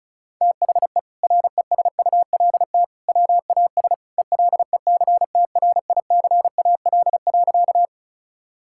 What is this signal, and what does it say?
Morse code, heard as a keyed tone, about 36 wpm, 700 Hz. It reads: THE RESULT WAS ELECTRICAL.
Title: Sentences from Top 1000 Words - Review Part 6 - 35wpm